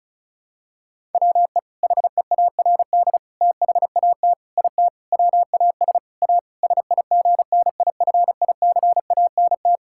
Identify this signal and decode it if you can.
Morse code, heard as a keyed tone, about 36 wpm, 700 Hz. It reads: WE HEARD THAT IT WAS A SIGNIFICANT
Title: Sentences from Top 1000 Words - Review Part 6 - 35wpm